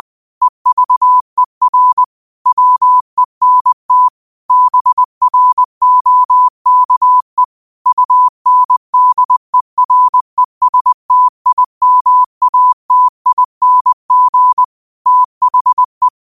Morse code, heard as a keyed tone, about 20 wpm, 1000 Hz. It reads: EVER WENT BROKE UNDERESTIMATING THE
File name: 20wpm